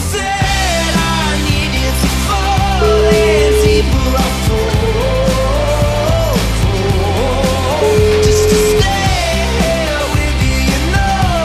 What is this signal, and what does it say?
Ringback tone (Europe, 425 Hz), with noise.